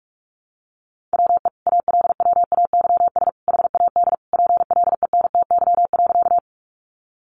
Morse code, heard as a keyed tone, about 44 wpm, 700 Hz. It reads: WE ALWAYS HAD PLENTY.
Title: Sentences from Top 1300 Words 45wpm